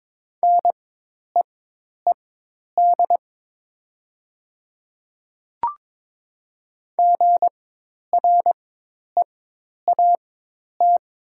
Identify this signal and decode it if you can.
Morse code, heard as a keyed tone, about 22 wpm, spaced as for 10 wpm, 700 Hz. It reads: NEED GREAT
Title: 4x Character Spacing - Top 300 Words - 22wpm